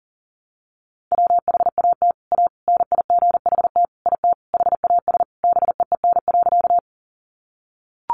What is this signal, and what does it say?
Morse code, heard as a keyed tone, about 40 wpm, 700 Hz. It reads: WHAT A NIGHT IT HAS BEEN.
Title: Sentences from Top 300 Words 40wpm